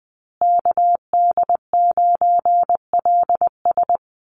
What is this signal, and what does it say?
Morse code, heard as a keyed tone, about 20 wpm, 700 Hz. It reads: KD9LS
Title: USA Call Signs Repeated 2x 20wpm